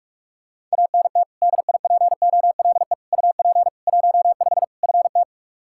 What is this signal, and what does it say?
Morse code, heard as a keyed tone, about 45 wpm, 700 Hz. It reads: ANT DIPOLE UP 15 FT
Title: Practice QSO - Exchange 2 - Info - 45wpm